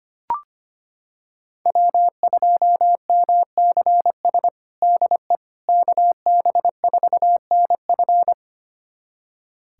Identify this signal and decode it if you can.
Morse code, heard as a keyed tone, about 25 wpm, 700 Hz. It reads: W2MCS DE KB4NF
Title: USA Callsign DE Callsign - 5 Characters - 25wpm